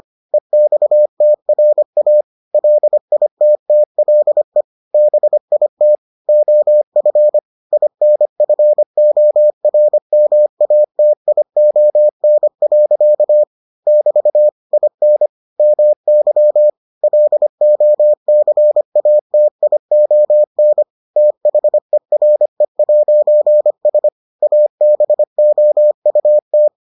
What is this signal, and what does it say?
Morse code, heard as a keyed tone, about 25 wpm, 600 Hz. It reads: EXTRA LITTLE BIT OF INFORMATION. <BT> IN MY LOCATION THERE'S ABOUT